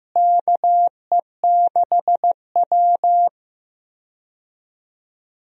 Morse code, heard as a keyed tone, about 15 wpm, 700 Hz. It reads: KE6W
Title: USA Call Signs - 4 Characters - Repeated 2x - 15wpm